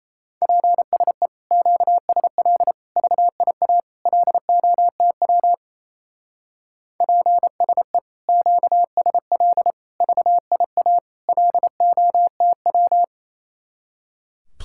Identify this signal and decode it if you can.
Morse code, heard as a keyed tone, about 35 wpm, 700 Hz. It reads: PSE QSL VIA LOTW PSE QSL VIA LOTW
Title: Speed-Racing Common QSO Phrases - 28wpm